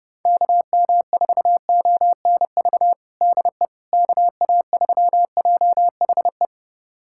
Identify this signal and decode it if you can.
Morse code, heard as a keyed tone, about 30 wpm, 700 Hz. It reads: KM4ONV DE KA3JHE
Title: USA Callsign DE Callsign - 6 Characters - 30wpm